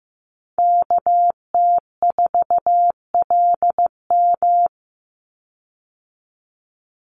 Morse code, heard as a keyed tone, about 15 wpm, 700 Hz. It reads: KT4LM
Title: USA Call Signs 15wpm